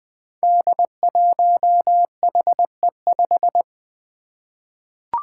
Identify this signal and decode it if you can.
Morse code, heard as a keyed tone, about 20 wpm, 700 Hz. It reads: D1HE5
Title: Five Challenging Letter-Number Combinations 20wpm